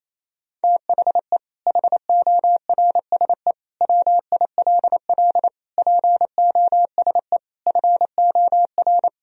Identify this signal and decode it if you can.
Morse code, heard as a keyed tone, about 28 wpm, 700 Hz. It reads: THE HORSE WILL POSE FOR
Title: Sentences from Top 1000 Words - Review Part 2 - 28wpm